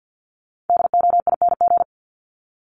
Morse code, heard as a keyed tone, about 50 wpm, 700 Hz. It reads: DOING
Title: ING Suffix - Top 100 Words with Base Words 50wpm